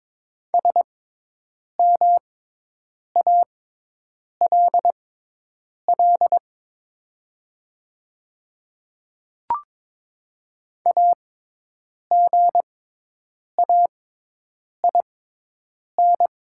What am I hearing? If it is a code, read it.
Morse code, heard as a keyed tone, about 22 wpm, spaced as for 8 wpm, 700 Hz. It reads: SMALL AGAIN
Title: 6x Character Spacing - Top 200 Words - 22wpm